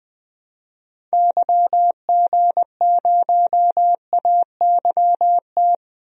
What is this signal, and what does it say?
Morse code, heard as a keyed tone, about 21 wpm, 700 Hz. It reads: YG0AYT
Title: DXCC-100 Call Signs - 6 Characters - Repeated 3x - 20wpm